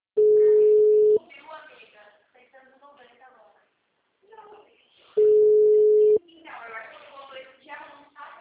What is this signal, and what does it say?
Ringback tone (Europe, 425 Hz), with speech.
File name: out- -2002- - - .30